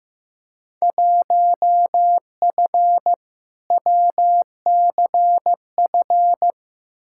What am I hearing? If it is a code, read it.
Morse code, heard as a keyed tone, about 15 wpm, 700 Hz. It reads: 1F WCF
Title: Field Day Exchanges - 15wpm